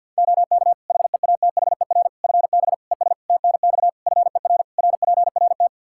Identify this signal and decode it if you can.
Morse code, heard as a keyed tone, about 51 wpm, 700 Hz. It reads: OK HEATHER FB ES TNX FER RPRT
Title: K1USN SST QSO Practice 50wpm